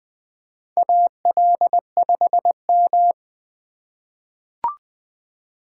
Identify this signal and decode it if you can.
Morse code, heard as a keyed tone, about 20 wpm, 700 Hz. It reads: AL5M
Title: USA Call Signs - 4x Character Spacing - 20wpm